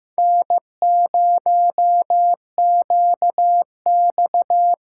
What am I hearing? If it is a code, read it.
Morse code, heard as a keyed tone, about 15 wpm, 700 Hz. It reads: N0QX
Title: USA Call Signs - 5 Characters - 8x Character Spacing - 15wpm